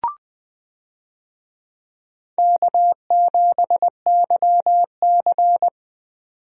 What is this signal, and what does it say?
Morse code, heard as a keyed tone, about 21 wpm, 700 Hz. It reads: K7YC
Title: USA Call Signs - 4 Characters - 20wpm